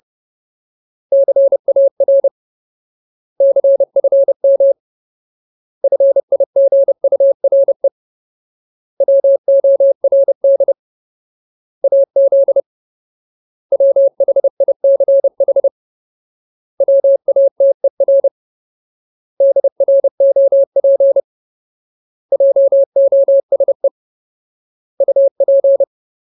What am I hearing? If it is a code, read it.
Morse code, heard as a keyed tone, about 30 wpm, 550 Hz. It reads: CAR CFM FIGURE WORD AZ WHICH WATER DROP JOSE UP
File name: Ultimate Word List / 30WPM Ultimate